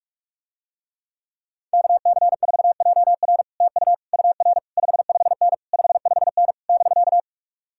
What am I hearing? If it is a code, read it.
Morse code, heard as a keyed tone, about 46 wpm, 700 Hz. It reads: KC4JR TU UR 55N 55N <BK>